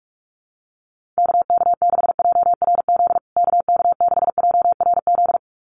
Morse code, heard as a keyed tone, about 44 wpm, 700 Hz. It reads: KK6JRZ KK6JRZ
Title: USA Call Signs Repeated 2x 45wpm